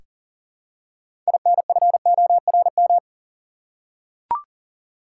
Morse code, heard as a keyed tone, about 40 wpm, 700 Hz. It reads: INFORM
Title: Six Letter Words - 40wpm